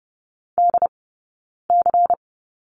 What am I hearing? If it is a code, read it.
Morse code, heard as a keyed tone, about 30 wpm, 700 Hz. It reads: D C
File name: Lesson 16d - Letter c - Two Characters 30wpm